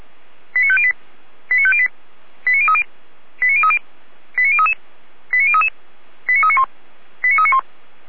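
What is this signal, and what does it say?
Five-tone selective call (ZVEI), heard as quick runs of five tones, tones of about 70 ms. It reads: 89508 89508 89030 89030 89033 89033 89391 89391
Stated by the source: Five-tone ZVEI selective call 89042, received on SDR.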